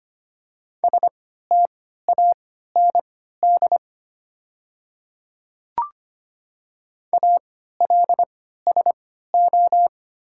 Morse code, heard as a keyed tone, about 25 wpm, spaced as for 14 wpm, 700 Hz. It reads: STAND ALSO
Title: 3x Character Spacing - Top 300 Words - 25wpm